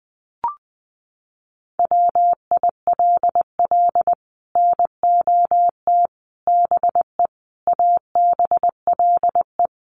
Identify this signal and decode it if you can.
Morse code, heard as a keyed tone, about 20 wpm, 700 Hz. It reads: WILL NOT BE ABLE
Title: Sets of 5 Words 20wpm